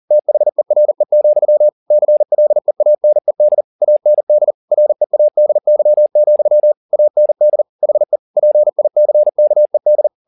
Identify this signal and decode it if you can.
Morse code, heard as a keyed tone, about 40 wpm, 600 Hz. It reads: THERE, CLEANED AND READY, AND HE PICKED
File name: the old man and the sea 2 40wpm